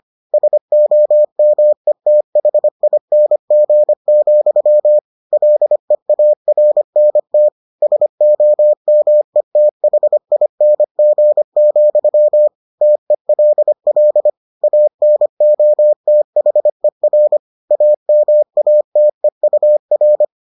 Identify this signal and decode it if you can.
Morse code, heard as a keyed tone, about 25 wpm, 600 Hz. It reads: SOMETHING, LEARNT SOMETHING, TELL ANOTHER AMATEUR